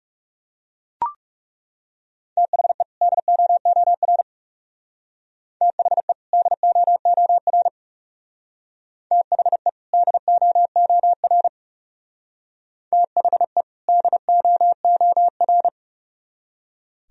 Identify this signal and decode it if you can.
Morse code, heard as a keyed tone, about 37 wpm, 700 Hz. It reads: THE DOOR THE DOOR THE DOOR THE DOOR
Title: Speed-Racing - Sets of 2 Words 30wpm